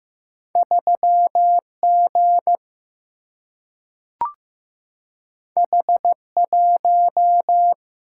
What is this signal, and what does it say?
Morse code, heard as a keyed tone, about 15 wpm, 700 Hz. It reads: 3G H1
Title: Single-Letter Single-Number 15wpm